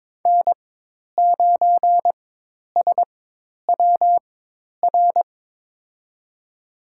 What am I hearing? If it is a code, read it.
Morse code, heard as a keyed tone, about 22 wpm, spaced as for 10 wpm, 700 Hz. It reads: N9SWR
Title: USA Call Signs - 4x Character Spacing - 22wpm